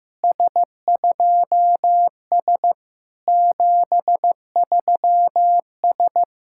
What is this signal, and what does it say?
Morse code, heard as a keyed tone, about 14 wpm, 700 Hz. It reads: S2S 73S
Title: SOTA QSO Practice 15wpm